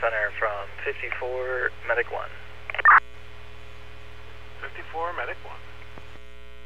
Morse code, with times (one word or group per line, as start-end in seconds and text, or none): none